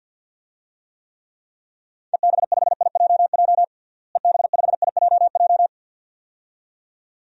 Morse code, heard as a keyed tone, about 49 wpm, 700 Hz. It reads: EB5IJJ EB5IJJ
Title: DXCC-100 Call Signs - 6 Characters - Repeated 2x - 50wpm